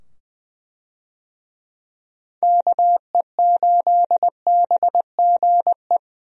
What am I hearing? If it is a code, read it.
Morse code, heard as a keyed tone, about 20 wpm, 700 Hz. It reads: KE8BGE
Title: USA Call Signs - 6 Characters - 20wpm